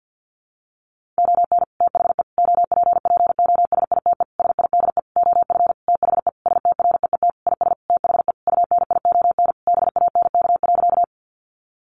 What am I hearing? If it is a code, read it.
Morse code, heard as a keyed tone, about 50 wpm, 700 Hz. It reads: ON THE OPPOSITE SIDE OF THE STREET IS THE UNION BANK.
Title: Sentences from Top 1400 Words 50wpm